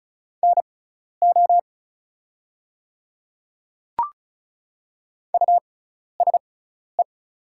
Morse code, heard as a keyed tone, about 36 wpm, spaced as for 12 wpm, 700 Hz. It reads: NO USE
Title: 6x Character Spacing - Top 100 Words - 35wpm